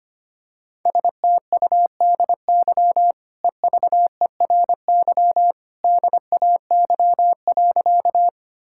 Morse code, heard as a keyed tone, about 25 wpm, 690 Hz. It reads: STUDY EVERY DAY.
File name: Easy Sentences with Hints - 25wpm